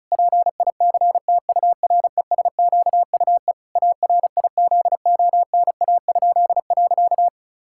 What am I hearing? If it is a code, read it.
Morse code, heard as a keyed tone, about 34 wpm, 700 Hz. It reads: PICTURESQUE ARIZONA?.
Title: Pangram Challenge v2 - 35wpm